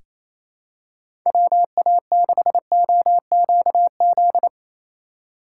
Morse code, heard as a keyed tone, about 28 wpm, 700 Hz. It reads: WA6OQZ
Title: USA Call Signs - 6x Character Spacing - 28wpm